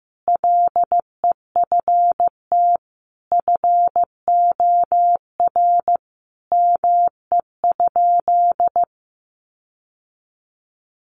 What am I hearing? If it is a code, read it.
Morse code, heard as a keyed tone, about 15 wpm, 700 Hz. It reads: LEFT FOR ME?
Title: Sentences from Top 1500 Words 15wpm